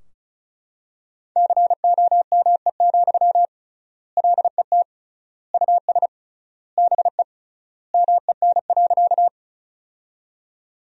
Morse code, heard as a keyed tone, about 36 wpm, 700 Hz. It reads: COME, LET US BE MEN.
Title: 3x Word-Spacing - Sentences from Top 300 Words - 35wpm